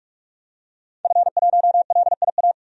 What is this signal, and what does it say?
Morse code, heard as a keyed tone, about 45 wpm, 700 Hz. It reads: F1LIA
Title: DXCC-100 Call Signs - 5 Characters - 45wpm